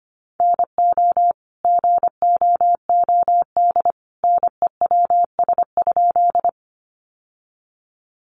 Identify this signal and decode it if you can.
Morse code, heard as a keyed tone, about 25 wpm, 700 Hz. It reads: NO GOOD NEWS?
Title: Sentences from Top 800 Words 25wpm